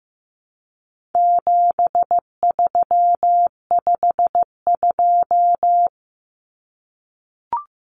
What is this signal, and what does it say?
Morse code, heard as a keyed tone, about 15 wpm, 700 Hz. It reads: 7352
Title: Numbers - 4-Digit 15wpm